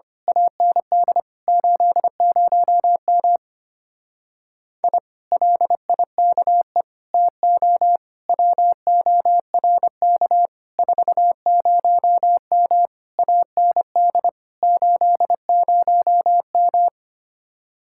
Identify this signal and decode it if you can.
Morse code, heard as a keyed tone, about 24 wpm, 700 Hz. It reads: AND 80M I LIKE TO WORK 40M AND 80M
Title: Speed-Racing Common QSO Phrases - 25wpm